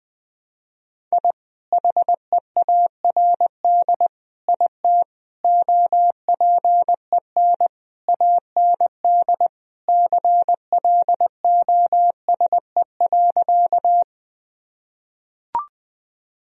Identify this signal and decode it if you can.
Morse code, heard as a keyed tone, about 21 wpm, 700 Hz. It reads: I HEARD IT OPEN AND CLOSE.
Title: Sentences from Top 1000 Words - Review Part 1 - 20wpm